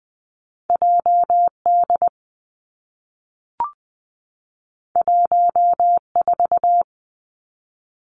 Morse code, heard as a keyed tone, about 19 wpm, 700 Hz. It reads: J D 1 4
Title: Two Challenging Letter-Number Combinations 20wpm